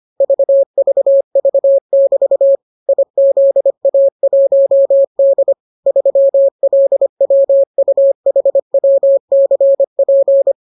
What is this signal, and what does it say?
Morse code, heard as a keyed tone, about 25 wpm, 550 Hz. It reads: VVV<BT> IZA1D 3LWUHWCP